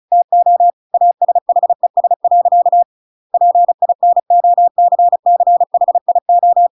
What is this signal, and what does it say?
Morse code, heard as a keyed tone, about 34 wpm, 700 Hz. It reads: TO ASHES. PINOCCHIO